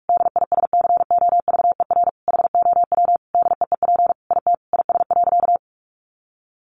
Morse code, heard as a keyed tone, about 46 wpm, 700 Hz. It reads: DISCOVER HOW DEEP IT IS.
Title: Sentences from Top 1100 Words 45wpm